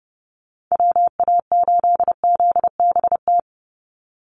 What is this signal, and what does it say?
Morse code, heard as a keyed tone, about 30 wpm, 700 Hz. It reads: WA8ZBT
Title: Call Signs Repeated 3x - 30wpm